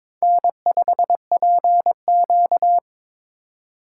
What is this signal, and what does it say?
Morse code, heard as a keyed tone, about 22 wpm, 700 Hz. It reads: N5PQ
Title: USA Call Signs - 2x Character Spacing - 22wpm